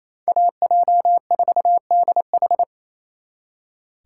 Morse code, heard as a keyed tone, about 33 wpm, 700 Hz. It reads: AJ4DH
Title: Speed Racing - USA 2x2 Call Signs 28wpm